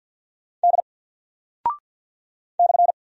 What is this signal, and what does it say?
Morse code, heard as a keyed tone, about 50 wpm, 700 Hz. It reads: D /
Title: Single Character (Letter:Number:Punctuation:Prosign) 50wpm